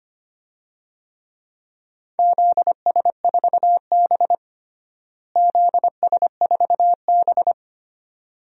Morse code, heard as a keyed tone, about 25 wpm, 700 Hz. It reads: ZS4B ZS4B
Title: DXCC-100 Call Signs - 4 Characters - Repeated 2x - 25wpm